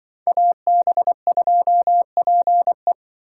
Morse code, heard as a keyed tone, about 24 wpm, 700 Hz. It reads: AB2PE
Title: Speed Racing - USA Call Signs 20wpm